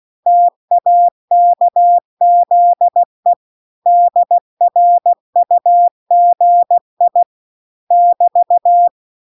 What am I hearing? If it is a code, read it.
Morse code, heard as a keyed tone, about 16 wpm, 700 Hz. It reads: TAKZE DRUGI <BT>